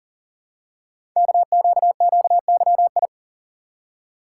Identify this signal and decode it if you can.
Morse code, heard as a keyed tone, about 41 wpm, 690 Hz. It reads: KQQYI
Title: Code Groups - 5 Letter:Numbers - 40wpm